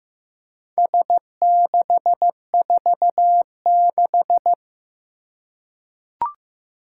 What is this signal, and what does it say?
Morse code, heard as a keyed tone, about 15 wpm, 700 Hz. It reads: S646
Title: Four Challenging Letter-Number Combinations 15wpm